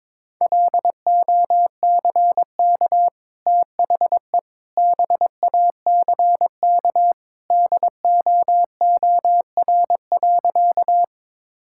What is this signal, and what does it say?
Morse code, heard as a keyed tone, about 22 wpm, 700 Hz. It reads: LOCK THE BACK DOOR.
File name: practice / 22wpm / Easy Sentences with Hints - 22wpm